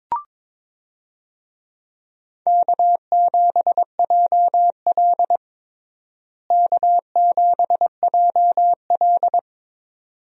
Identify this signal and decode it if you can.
Morse code, heard as a keyed tone, about 22 wpm, 700 Hz. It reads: K7JL K7JL
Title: USA Call Signs - 4 Characters - Repeated 2x - 22wpm